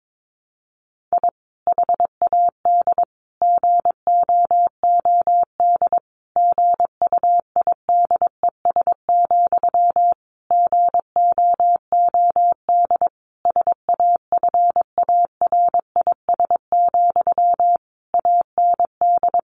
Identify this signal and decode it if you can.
Morse code, heard as a keyed tone, about 22 wpm, 700 Hz. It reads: I HAD GOOD GUIDES, GOOD SAFARIS, AND
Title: A Sound Of Thunder 22wpm